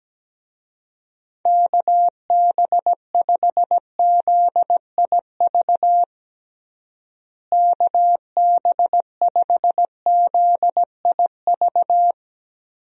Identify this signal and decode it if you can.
Morse code, heard as a keyed tone, about 17 wpm, 700 Hz. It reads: KB5ZIV KB5ZIV
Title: USA Call Signs - 6 Characters - Repeated 3x - 17wpm